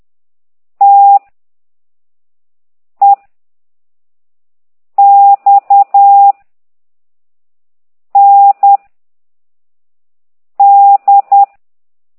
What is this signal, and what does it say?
Morse code, heard as a keyed tone, about 10 wpm, 800 Hz. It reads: T E X N D
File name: ct2idw / exerc9